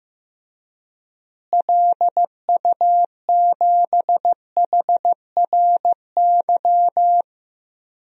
Morse code, heard as a keyed tone, about 15 wpm, 700 Hz. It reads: LU7HRY
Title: DXCC-100 Call Signs - 6 Characters - Repeated 3x - 15wpm